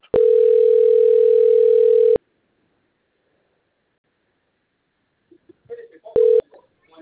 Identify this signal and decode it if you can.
Ringback tone (North America), 440 + 480 Hz.